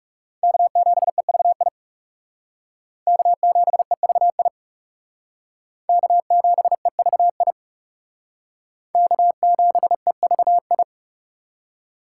Morse code, heard as a keyed tone, about 37 wpm, 700 Hz. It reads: K7EVI K7EVI K7EVI K7EVI
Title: Speed Racing - USA 1x3 Call Signs 30wpm